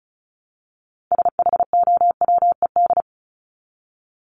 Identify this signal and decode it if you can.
Morse code, heard as a keyed tone, about 34 wpm, 700 Hz. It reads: SHOWED
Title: ED Suffix - Top 100 Words with Base Words 35wpm